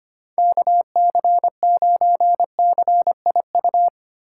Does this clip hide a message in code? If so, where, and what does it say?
Yes: Morse code from 0.4 to 3.9 s, heard as a keyed tone, about 32 wpm, 700 Hz. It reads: KC9CIU